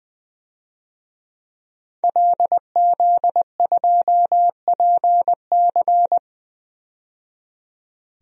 Morse code, heard as a keyed tone, about 20 wpm, 700 Hz. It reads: LZ2PC
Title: DXCC-100 Call Signs - 5 Characters - 20wpm